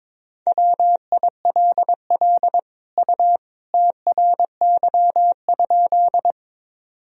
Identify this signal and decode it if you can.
Morse code, heard as a keyed tone, about 21 wpm, 700 Hz. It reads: WILL U TRY?
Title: Sentences from Top 200 Words and Abbreviations 22wpm